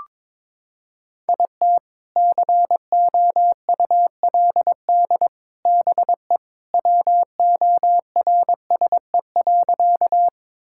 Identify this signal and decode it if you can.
Morse code, heard as a keyed tone, about 21 wpm, 700 Hz. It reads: IT COULD BE WORSE.
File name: practice / 22wpm / Sentences with 3-Word Phrases - 22wpm